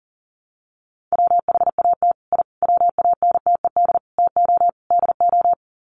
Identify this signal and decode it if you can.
Morse code, heard as a keyed tone, about 37 wpm, 700 Hz. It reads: WHAT I WANTED TO DO
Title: Sets of 5 Words Encore2 40wpm